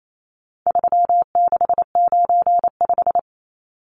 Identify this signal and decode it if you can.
Morse code, heard as a keyed tone, about 28 wpm, 700 Hz. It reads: 3695
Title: Numbers - 4-Digit 28wpm